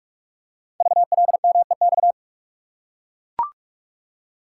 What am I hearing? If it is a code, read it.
Morse code, heard as a keyed tone, about 44 wpm, 690 Hz. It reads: VLMEX
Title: Code Groups - 5 Letters - 45wpm